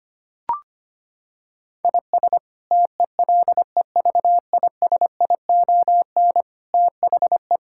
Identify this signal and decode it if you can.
Morse code, heard as a keyed tone, about 25 wpm, 700 Hz. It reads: IS TELEVISION THE
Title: Sentences from Top 1000 Words - Review Part 6 - 25wpm